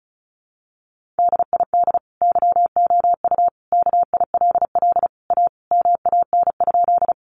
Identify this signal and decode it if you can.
Morse code, heard as a keyed tone, about 34 wpm, 700 Hz. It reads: DID YOU KILL A MAN?
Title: Sentences from Top 1400 Words 35wpm